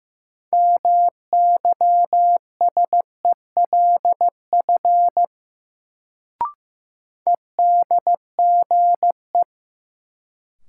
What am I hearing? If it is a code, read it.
Morse code, heard as a keyed tone, about 15 wpm, 700 Hz. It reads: MYSELF EDGE
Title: Sentences from Top 1300 Words - New Word Drill Alt - 15wpm